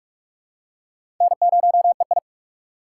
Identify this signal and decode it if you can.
Morse code, heard as a keyed tone, about 47 wpm, 700 Hz. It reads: N0EI
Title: USA Call Signs - 4 Characters - 3x Character Spacing - 45wpm